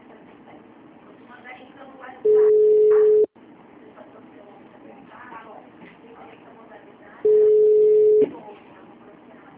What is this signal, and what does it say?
Ringback tone (Europe, 425 Hz), with speech.
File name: out- -2030- - - .13131